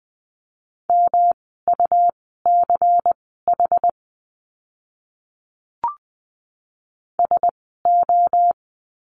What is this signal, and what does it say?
Morse code, heard as a keyed tone, about 20 wpm, spaced as for 14 wpm, 700 Hz. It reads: MUCH SO
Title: 230 Lesson 37f - Letter x - Words 20wpm